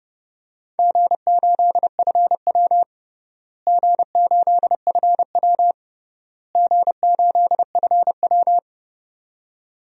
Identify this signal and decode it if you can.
Morse code, heard as a keyed tone, about 32 wpm, 700 Hz. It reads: G8FW G8FW G8FW
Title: DXCC-100 Call Signs - 4 Characters - Repeated 3x - 30wpm